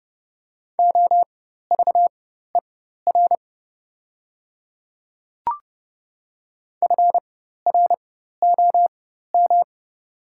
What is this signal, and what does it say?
Morse code, heard as a keyed tone, about 30 wpm, spaced as for 14 wpm, 700 Hz. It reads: OVER FROM
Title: 4x Character Spacing - Top 500 Words - 30wpm